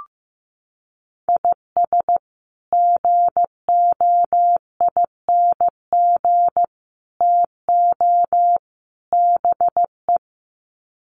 Morse code, heard as a keyed tone, about 15 wpm, 700 Hz. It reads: IS GOING TO BE
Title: Sets of 4 Words 15wpm